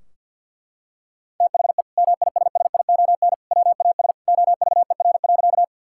Morse code, heard as a keyed tone, about 50 wpm, 700 Hz. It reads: THE MISSION WAS OVER.
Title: Sentences from Top 1600 Words 50wpm